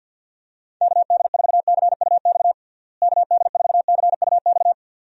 Morse code, heard as a keyed tone, about 46 wpm, 700 Hz. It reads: KD4CUX KD4CUX
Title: USA Call Signs - 6 Characters - Repeated 2x - 50wpm